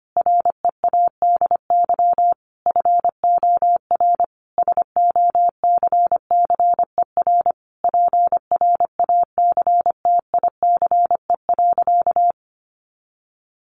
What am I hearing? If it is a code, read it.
Morse code, heard as a keyed tone, about 25 wpm, 700 Hz. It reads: READY FOR SOCCER PRACTICE.
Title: Easy Phrases 25wpm